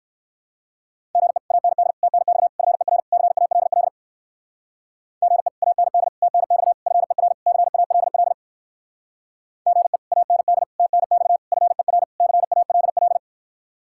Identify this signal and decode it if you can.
Morse code, heard as a keyed tone, about 56 wpm, 700 Hz. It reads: GE AND TNX FER CALL GE AND TNX FER CALL GE AND TNX FER CALL